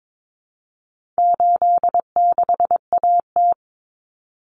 Morse code, heard as a keyed tone, about 22 wpm, 700 Hz. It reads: 86AT
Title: Common Cut Numbers - 4-Digit 22wpm